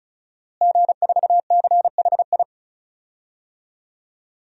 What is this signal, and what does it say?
Morse code, heard as a keyed tone, about 34 wpm, 700 Hz. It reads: G4CHI
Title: DXCC-100 Call Signs - 5 Characters - Repeated 2x - 35wpm